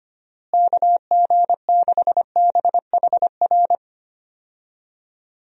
Morse code, heard as a keyed tone, about 25 wpm, 700 Hz. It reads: KG6BHR
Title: USA Call Signs - 6 Characters - Repeated 3x - 25wpm